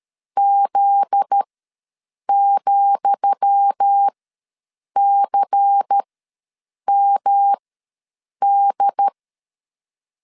Morse code, heard as a keyed tone, about 13 wpm, spaced as for 7 wpm, 800 Hz. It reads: Z,CMD
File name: LEZIONI / 10 SIGNS / 10D COMA ws 12wpm 3618